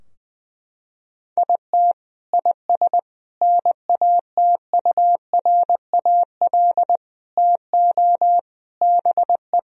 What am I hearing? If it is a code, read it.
Morse code, heard as a keyed tone, about 20 wpm, 700 Hz. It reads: IT IS NATURAL TO BE
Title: Sentences from Top 1000 Words - Review Part 4 - 20wpm